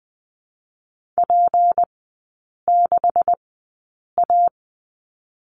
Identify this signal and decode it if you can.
Morse code, heard as a keyed tone, about 20 wpm, 700 Hz. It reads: P 6 A
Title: 172 Lesson 28e - Number 9 - Three Characters 20wpm